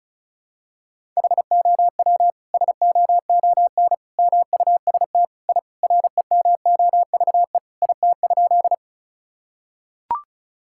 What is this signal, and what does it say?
Morse code, heard as a keyed tone, about 36 wpm, 700 Hz. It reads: HOW SOON MUST I REMOVE IT?
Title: Sentences from Top 1000 Words - Review Part 5 - 35wpm